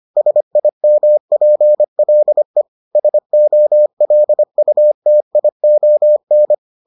Morse code, heard as a keyed tone, about 25 wpm, 600 Hz. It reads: SIMPLE SOLUTION